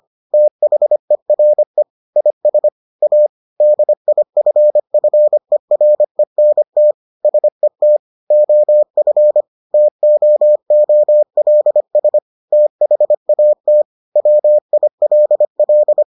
Morse code, heard as a keyed tone, about 25 wpm, 600 Hz. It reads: THERE IS A DIFFERENT SET OF TOOLS THAT WILL